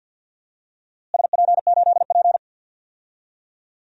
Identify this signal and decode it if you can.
Morse code, heard as a keyed tone, about 48 wpm, 700 Hz. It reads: SP8P
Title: DXCC-100 Call Signs - 4 Characters - Repeated 2x - 50wpm